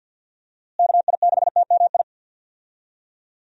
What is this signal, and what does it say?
Morse code, heard as a keyed tone, about 46 wpm, 700 Hz. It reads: KI6TMI
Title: USA Call Signs - 6 Characters - Repeated 3x - 50wpm